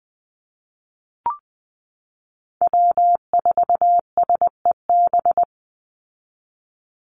Morse code, heard as a keyed tone, about 20 wpm, 700 Hz. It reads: W4SEB
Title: USA Call Signs 20wpm